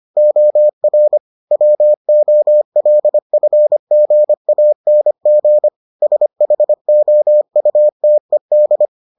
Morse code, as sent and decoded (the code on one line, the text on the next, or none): --- .-. / .-- --- .-.. ..-. --. .- -. --. / ... .... --- ..- - . -..
OR WOLFGANG SHOUTED